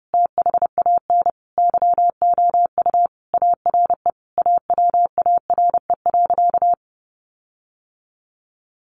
Morse code, heard as a keyed tone, about 30 wpm, 700 Hz. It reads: THAN YOU ARE AWARE.
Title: Sentences from Top 1300 Words 30wpm